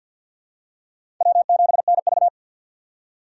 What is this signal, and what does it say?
Morse code, heard as a keyed tone, about 50 wpm, 700 Hz. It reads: W7NV
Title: USA Call Signs - 4 Characters - 6x Character Spacing - 50wpm